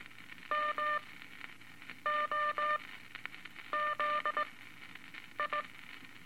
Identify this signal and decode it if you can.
Morse code, heard as a keyed tone, about 19 wpm, 1200 Hz, with noise. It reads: M O Z I
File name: Morse Code Records / 17A LCM 162 Eight Words Per Minute